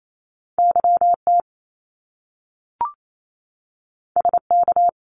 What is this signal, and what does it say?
Morse code, heard as a keyed tone, about 28 wpm, 700 Hz. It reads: YT SK
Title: Canadian Provinces and Territories 28wpm